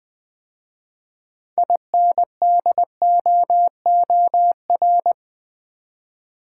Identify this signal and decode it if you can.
Morse code, heard as a keyed tone, about 20 wpm, 700 Hz. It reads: INDOOR
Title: 6 Letter Words - First 3 Letters Given - 20wpm